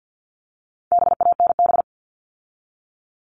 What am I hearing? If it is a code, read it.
Morse code, heard as a keyed tone, about 50 wpm, 700 Hz. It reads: BANB
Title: All Cut Numbers - 4-Digit 50wpm